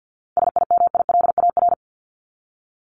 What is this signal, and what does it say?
Morse code, heard as a keyed tone, about 50 wpm, 700 Hz. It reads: SIMILAR
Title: LY Suffix - Top 100 Words with Base Words 50wpm